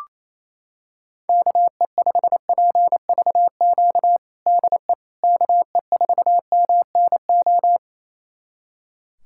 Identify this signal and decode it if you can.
Morse code, heard as a keyed tone, about 28 wpm, 700 Hz. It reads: KE5PVQ DE KE4MNO
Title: USA Callsign DE Callsign 28wpm